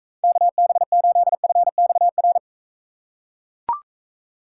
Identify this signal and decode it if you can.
Morse code, heard as a keyed tone, about 43 wpm, 700 Hz. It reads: KB8FXR